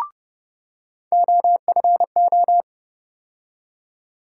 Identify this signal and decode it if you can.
Morse code, heard as a keyed tone, about 30 wpm, 700 Hz. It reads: OFO
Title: USA Call Sign Postfixes - 3 Letters 30wpm